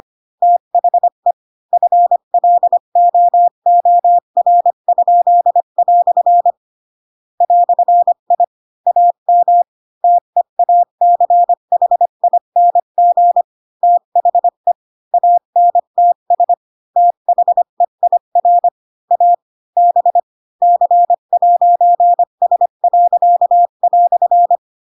Morse code, heard as a keyed tone, about 25 wpm, 700 Hz. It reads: THE FLOOR?" "I AM TEACHING THE ANTS THEIR A B C'S."